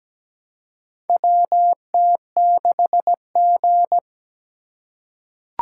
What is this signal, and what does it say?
Morse code, heard as a keyed tone, about 17 wpm, 700 Hz. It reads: WT6G
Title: USA Call Signs - 2x Character Spacing - 17wpm